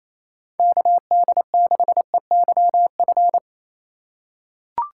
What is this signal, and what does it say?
Morse code, heard as a keyed tone, about 28 wpm, 700 Hz. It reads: KD6EYF
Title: USA Call Signs - 4x Character Spacing - 28wpm